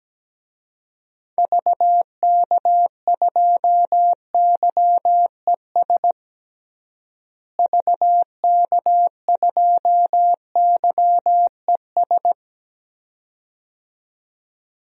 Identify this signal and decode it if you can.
Morse code, heard as a keyed tone, about 17 wpm, 700 Hz. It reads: VK2YES VK2YES
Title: DXCC-100 Call Signs - 6 Characters - Repeated 2x - 17wpm